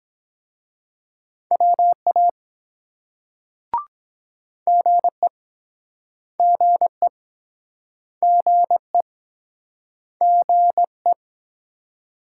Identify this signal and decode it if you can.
Morse code, heard as a keyed tone, about 23 wpm, 700 Hz. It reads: WA GE GE GE GE